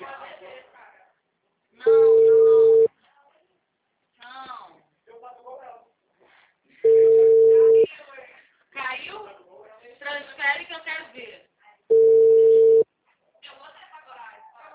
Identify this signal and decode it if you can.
Ringback tone (China, 450 Hz), with speech.
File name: out- -2007- - - .87159